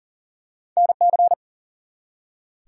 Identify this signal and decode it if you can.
Morse code, heard as a keyed tone, about 42 wpm, 700 Hz. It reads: NC
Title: US States and Canadian Provinces 40wpm